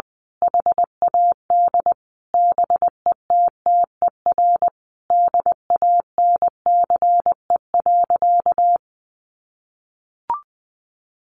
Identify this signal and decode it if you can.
Morse code, heard as a keyed tone, about 20 wpm, 700 Hz. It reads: HAD BETTER DANCE.
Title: Sentences from Top 700 Words 20wpm